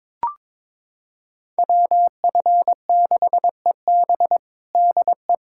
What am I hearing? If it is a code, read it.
Morse code, heard as a keyed tone, about 22 wpm, 700 Hz. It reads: WF6EB DE
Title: USA Callsign DE Callsign - 5 Characters - 22wpm